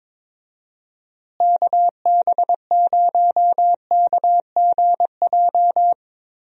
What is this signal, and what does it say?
Morse code, heard as a keyed tone, about 22 wpm, 700 Hz. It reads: KB0KGJ
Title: USA Call Signs - 6 Characters - Repeated 3x - 22wpm